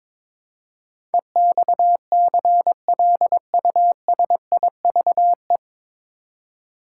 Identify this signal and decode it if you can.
Morse code, heard as a keyed tone, about 22 wpm, 700 Hz. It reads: EXCLUSIVE
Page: Morse Code Ninja practice 22wpm